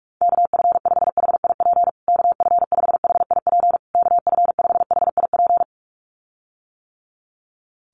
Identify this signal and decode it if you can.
Morse code, heard as a keyed tone, about 44 wpm, 700 Hz. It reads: KF5HIP KF5HIP KF5HIP
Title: USA Call Signs Repeated 3x 45wpm